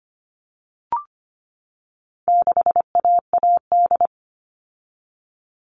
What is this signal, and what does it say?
Morse code, heard as a keyed tone, about 25 wpm, 700 Hz. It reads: 6AAD
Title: All Cut Numbers - 4-Digit 25wpm